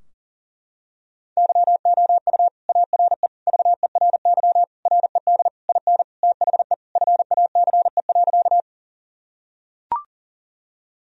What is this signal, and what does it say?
Morse code, heard as a keyed tone, about 37 wpm, 700 Hz. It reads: YOU ARE VERY RED IN THE FACE.
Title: Sentences from Top 1000 Words - Review Part 2 - 40wpm